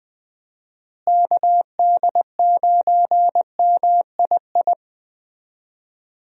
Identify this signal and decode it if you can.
Morse code, heard as a keyed tone, about 19 wpm, 700 Hz. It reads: KD9MII
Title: USA Call Signs - 6 Characters - Repeated 2x - 20wpm